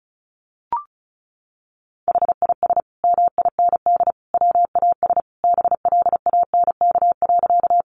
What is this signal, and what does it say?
Morse code, heard as a keyed tone, about 34 wpm, 700 Hz. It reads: HIS MIND WAS BLANK.
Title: Sentences from Top 1500 Words 35wpm